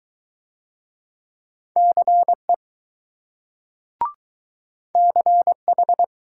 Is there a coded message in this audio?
Yes: Morse code from 1.8 to 6.0 s, heard as a keyed tone, about 21 wpm, 700 Hz. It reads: CE CH